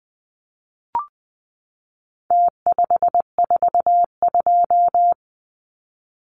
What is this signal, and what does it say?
Morse code, heard as a keyed tone, about 20 wpm, 700 Hz. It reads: T542
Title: Numbers - Contest Style - 4-Digit 20wpm